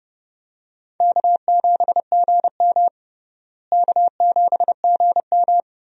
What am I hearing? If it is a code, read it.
Morse code, heard as a keyed tone, about 32 wpm, 700 Hz. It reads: K7GM K7GM
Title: USA Call Signs - 4 Characters - Repeated 2x - 30wpm